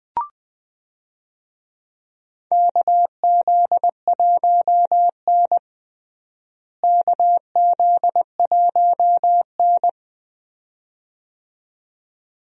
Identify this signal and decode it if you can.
Morse code, heard as a keyed tone, about 19 wpm, 700 Hz. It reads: KZ1N KZ1N
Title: USA Call Signs - 4 Characters - Repeated 2x - 20wpm